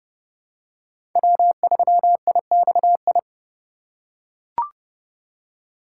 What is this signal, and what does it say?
Morse code, heard as a keyed tone, about 30 wpm, 700 Hz. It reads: W3IXI
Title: USA Call Signs - 5 Characters - 2x Character Spacing - 30wpm